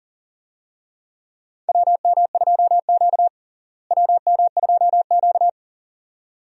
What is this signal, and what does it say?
Morse code, heard as a keyed tone, about 40 wpm, 700 Hz. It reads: WM2Q WM2Q
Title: USA Call Signs - 4 Characters - Repeated 2x - 40wpm